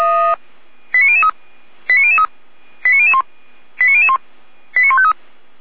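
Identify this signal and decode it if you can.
Five-tone selective call (ZVEI), heard as quick runs of five tones, tones of about 70 ms. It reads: 89003 89003 89001 89001 89153
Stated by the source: ZVEI five-tone selective call 89078 received on an SDR.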